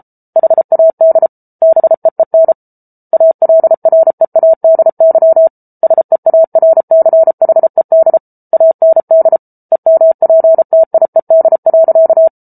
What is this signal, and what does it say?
Morse code, heard as a keyed tone, about 34 wpm, 650 Hz. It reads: HAD BEEN ALREADY SEARCHED AND EMPTIED.